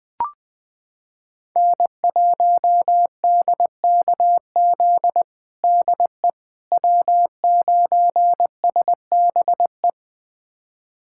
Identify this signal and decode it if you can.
Morse code, heard as a keyed tone, about 21 wpm, 700 Hz. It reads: N1DKZ DE W9SBE
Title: USA Callsign DE Callsign - 5 Characters - 20wpm